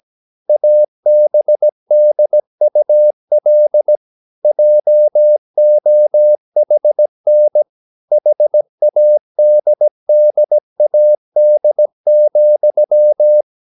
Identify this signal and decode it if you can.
Morse code, heard as a keyed tone, about 17 wpm, 600 Hz. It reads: ABDUL JOHN HADDAD,
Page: Romanen Never af Ken Follet på engelsk.